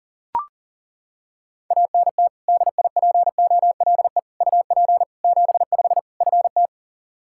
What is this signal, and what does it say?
Morse code, heard as a keyed tone, about 40 wpm, 700 Hz. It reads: ANT DIPOLE UP 75 FT
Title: K1USN SST QSO Practice 40wpm